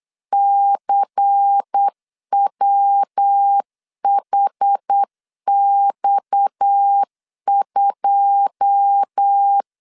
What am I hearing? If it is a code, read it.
Morse code, heard as a keyed tone, about 8 wpm, 800 Hz. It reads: CWHX2
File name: LEZIONI / 9 2 8wpm 3437